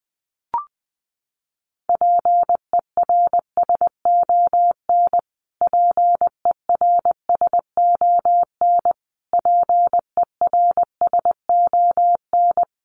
Morse code, heard as a keyed tone, about 19 wpm, 700 Hz. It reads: PERSON PERSON PERSON
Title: Top 500 Words Repeated 20wpm